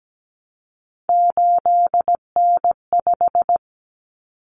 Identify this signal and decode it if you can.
Morse code, heard as a keyed tone, about 17 wpm, 700 Hz. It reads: 8N5
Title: Common Cut Numbers - 3-Digit 17wpm